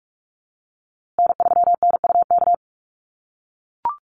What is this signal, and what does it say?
Morse code, heard as a keyed tone, about 44 wpm, 700 Hz. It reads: N3NUK
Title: USA Call Signs Repeated 2x 45wpm